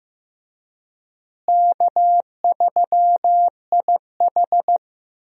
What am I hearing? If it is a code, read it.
Morse code, heard as a keyed tone, about 15 wpm, 700 Hz. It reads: K3IH
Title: USA Call Signs - 4 Characters - Repeated 3x - 15wpm